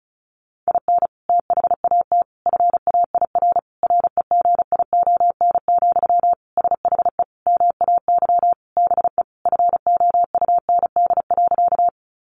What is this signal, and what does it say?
Morse code, heard as a keyed tone, about 34 wpm, 700 Hz. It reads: IN THAT FAIR REGION, SHE MAY BE FOUND.
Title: Sentences from Top 1500 Words 35wpm